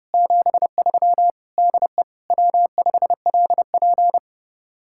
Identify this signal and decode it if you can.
Morse code, heard as a keyed tone, about 38 wpm, 700 Hz. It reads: 73 DE W5LP